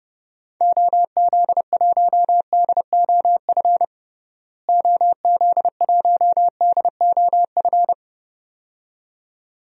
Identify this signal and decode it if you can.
Morse code, heard as a keyed tone, about 30 wpm, 700 Hz. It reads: OZ1DOF OZ1DOF
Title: DXCC-100 Call Signs - 6 Characters - Repeated 2x - 30wpm